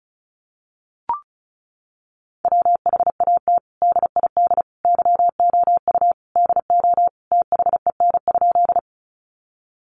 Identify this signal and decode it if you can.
Morse code, heard as a keyed tone, about 34 wpm, 700 Hz. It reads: WHAT DID YOU DO THEN?
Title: Sentences from Top 100 Words 35wpm